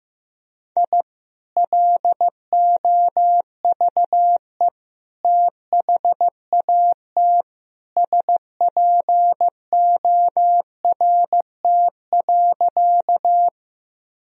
Morse code, heard as a keyed tone, about 15 wpm, 700 Hz. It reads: I LOVE THAT SPORT.
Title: Sentences from Top 1000 Words - Review Part 4 - 15wpm